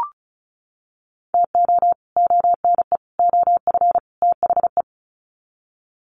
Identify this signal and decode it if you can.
Morse code, heard as a keyed tone, about 34 wpm, 700 Hz. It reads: TO ONE OF THE
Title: Sets of 4 Words 35wpm